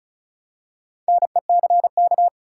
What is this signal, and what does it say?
Morse code, heard as a keyed tone, about 36 wpm, 700 Hz. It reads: NECK
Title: Four Letter Words - 35wpm